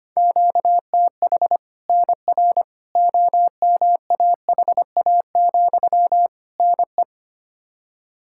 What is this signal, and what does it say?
Morse code, heard as a keyed tone, about 25 wpm, 700 Hz. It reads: QTH NR OMAHA, NE
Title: Practice QSO - Exchange 1 - Beginning - 25wpm